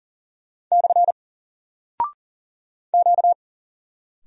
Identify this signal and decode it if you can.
Morse code, heard as a keyed tone, about 40 wpm, 700 Hz. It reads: / Q
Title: Single Character (Letter:Number:Punctuation:Prosign) 40wpm